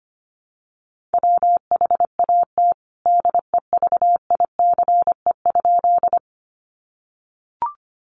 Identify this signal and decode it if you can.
Morse code, heard as a keyed tone, about 25 wpm, 700 Hz. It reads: WHAT DEVICE?
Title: Sentences from Top 1500 Words 25wpm